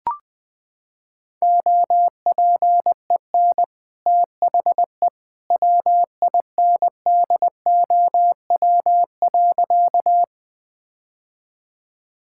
Morse code, heard as a keyed tone, about 20 wpm, 700 Hz. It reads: OPEN THE WINDOW.
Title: Sentences from Top 1600 Words 20wpm